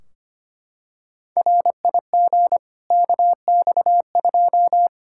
Morse code, heard as a keyed tone, about 25 wpm, 700 Hz. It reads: RIG KX2
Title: K1USN SST QSO Practice 25wpm